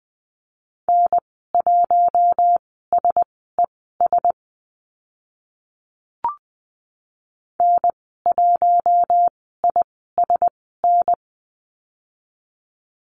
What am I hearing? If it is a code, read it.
Morse code, heard as a keyed tone, about 20 wpm, spaced as for 14 wpm, 700 Hz. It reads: N1SES N1ISN
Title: 062 Lesson 11g - Letter h - Call Signs 20wpm